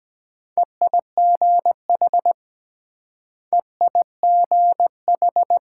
Morse code, heard as a keyed tone, about 23 wpm, 700 Hz. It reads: EIGH EIGH